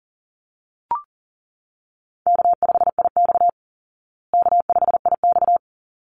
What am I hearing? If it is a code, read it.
Morse code, heard as a keyed tone, about 40 wpm, 700 Hz. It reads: K5IX K5IX
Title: Easy Call Signs Repeated 2x - 40wpm